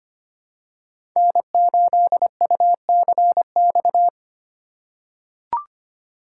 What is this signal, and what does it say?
Morse code, heard as a keyed tone, about 25 wpm, 700 Hz. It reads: N8UCX
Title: USA Call Signs - 6x Character Spacing - 25wpm